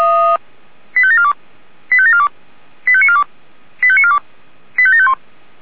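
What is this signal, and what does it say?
Five-tone selective call (ZVEI), heard as quick runs of five tones, tones of about 70 ms. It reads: 85742 85742 85942 85942 85751
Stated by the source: ZVEI five-tone selective call 85730 received by SDR.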